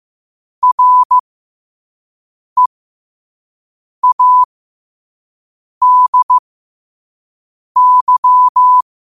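Morse code, heard as a keyed tone, about 15 wpm, spaced as for 5 wpm, 1000 Hz. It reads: READY
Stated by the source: Quote-of-the-day Morse code practice at 5 wpm.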